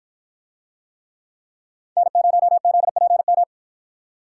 Morse code, heard as a keyed tone, about 53 wpm, 700 Hz. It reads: N0ZPM